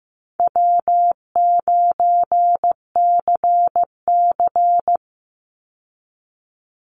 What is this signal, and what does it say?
Morse code, heard as a keyed tone, about 15 wpm, 700 Hz. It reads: W9CC
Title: Call Signs Repeated 2x - 15wpm